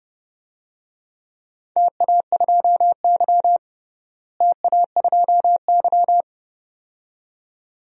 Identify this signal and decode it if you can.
Morse code, heard as a keyed tone, about 30 wpm, 700 Hz. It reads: TA2Y TA2Y
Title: DXCC-100 Call Signs - 4 Characters - Repeated 2x - 30wpm